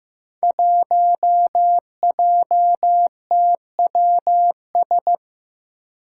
Morse code, heard as a keyed tone, about 15 wpm, 700 Hz. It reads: 1JTWS
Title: Five Challenging Letter-Number Combinations 15wpm